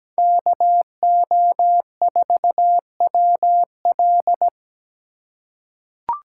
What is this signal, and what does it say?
Morse code, heard as a keyed tone, about 17 wpm, 700 Hz. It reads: KO4WL
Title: USA Call Signs - 5 Characters - 2x Character Spacing - 17wpm